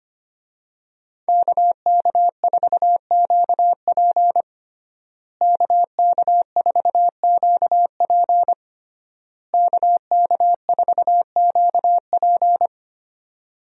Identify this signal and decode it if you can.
Morse code, heard as a keyed tone, about 25 wpm, 700 Hz. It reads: KK4QP KK4QP KK4QP
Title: USA Call Signs - 5 Characters - Repeated 3x - 25wpm